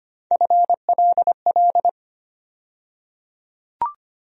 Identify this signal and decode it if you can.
Morse code, heard as a keyed tone, about 25 wpm, 700 Hz. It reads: FLL
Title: USA Call Sign Postfixes - 3 Letters 25wpm